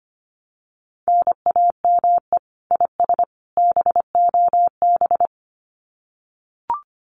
Morse code, heard as a keyed tone, about 25 wpm, 700 Hz. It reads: NAME IS BOB
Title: Common QSO Phrases 25wpm